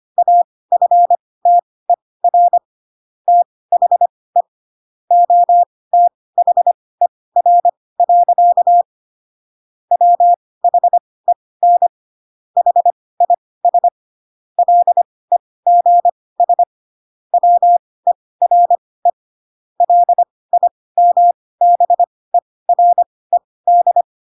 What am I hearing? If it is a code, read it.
Morse code, heard as a keyed tone, about 25 wpm, spaced as for 18 wpm, 700 Hz. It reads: AFTER THE OTHER. WHEN HIS LEGS WERE LIMBERED
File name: Pinocchio ch3 12 25 ik1hge